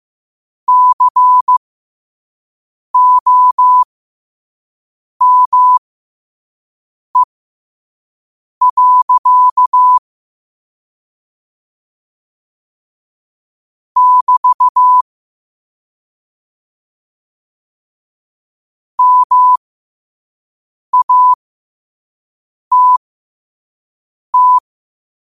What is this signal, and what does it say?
Morse code, heard as a keyed tone, about 15 wpm, spaced as for 5 wpm, 1000 Hz. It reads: COME. <BT> MATT
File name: qotdfiles / 5wpm